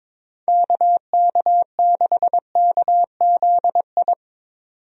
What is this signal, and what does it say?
Morse code, heard as a keyed tone, about 22 wpm, 700 Hz. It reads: KK6KZI
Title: USA Call Signs - 6 Characters - 22wpm